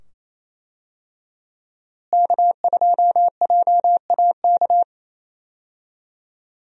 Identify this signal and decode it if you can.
Morse code, heard as a keyed tone, about 28 wpm, 700 Hz. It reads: K2JAK
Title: USA Call Signs - 5 Characters - Repeated 2x - 28wpm